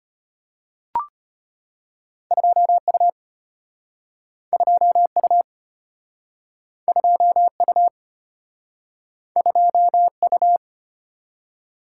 Morse code, heard as a keyed tone, about 35 wpm, 700 Hz. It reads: 2U 2U 2U 2U